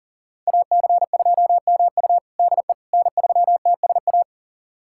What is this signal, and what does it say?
Morse code, heard as a keyed tone, about 40 wpm, 700 Hz. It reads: AC2MU DE N3TSA
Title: USA Callsign DE Callsign - 5 Characters - 40wpm